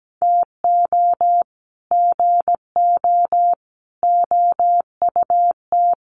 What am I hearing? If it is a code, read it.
Morse code, heard as a keyed tone, about 17 wpm, 700 Hz. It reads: TO GO OUT
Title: Sets of 5 Words Encore2 17wpm